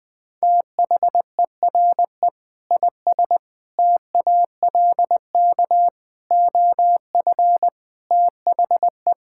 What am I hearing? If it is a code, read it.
Morse code, heard as a keyed tone, about 20 wpm, 700 Hz. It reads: THERE IS TALK OF THE
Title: Sentences with 5-Word Phrases - 20wpm